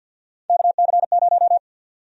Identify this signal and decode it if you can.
Morse code, heard as a keyed tone, about 52 wpm, 700 Hz. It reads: KC0
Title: USA Call Sign Prefixes - 50wpm